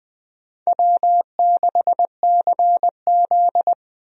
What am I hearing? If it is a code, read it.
Morse code, heard as a keyed tone, about 20 wpm, 700 Hz. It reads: W6CZ
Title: USA Call Signs - 4 Characters - 20wpm